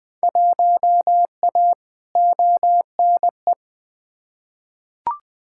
Morse code, heard as a keyed tone, about 20 wpm, 700 Hz. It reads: 1A ONE
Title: Field Day Exchanges - 20wpm